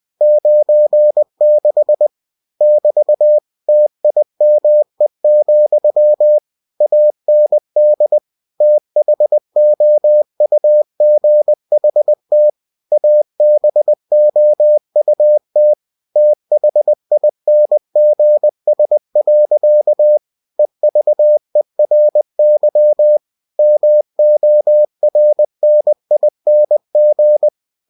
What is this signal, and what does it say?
Morse code, heard as a keyed tone, about 20 wpm, 600 Hz. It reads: 96 <BT> TIME, AND THOUGHT ABOUT THINGS. EVERY MORNING
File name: cw-books / winnie-the-pooh / winnie-20wpm-page-0096